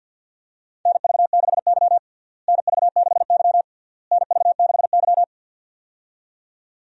Morse code, heard as a keyed tone, about 51 wpm, 700 Hz. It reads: NV6Y NV6Y NV6Y
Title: USA Call Signs - 4 Characters - Repeated 3x - 50wpm